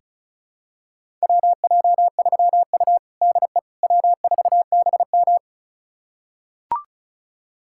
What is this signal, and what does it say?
Morse code, heard as a keyed tone, about 34 wpm, 700 Hz. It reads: WJ3U DE W4BM
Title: USA Callsign DE Callsign - 4 Characters - 35wpm